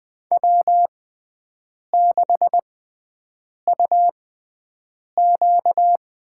Morse code, heard as a keyed tone, about 20 wpm, spaced as for 7 wpm, 700 Hz. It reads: W6UQ
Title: USA Call Signs - 5 Characters - 6x Character Spacing - 20wpm